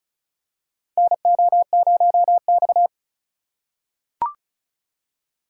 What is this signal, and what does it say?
Morse code, heard as a keyed tone, about 36 wpm, 700 Hz. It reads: NO0X
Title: USA Call Signs - 4 Characters - 3x Character Spacing - 35wpm